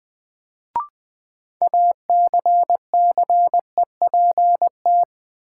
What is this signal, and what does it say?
Morse code, heard as a keyed tone, about 20 wpm, 700 Hz. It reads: ACCEPT
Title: Sentences from Top 1300 Words - New Word Drill - 20wpm